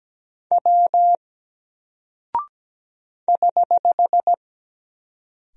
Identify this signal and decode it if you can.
Morse code, heard as a keyed tone, about 17 wpm, 700 Hz. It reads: W <HH>
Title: Single Character (Letter:Number:Punctuation:Prosign) Alt 17wpm